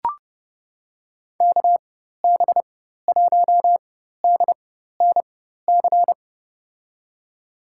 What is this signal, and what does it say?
Morse code, heard as a keyed tone, about 30 wpm, spaced as for 14 wpm, 700 Hz. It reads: KB1DNC
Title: USA Call Signs - 6 Characters - 4x Character Spacing - 30wpm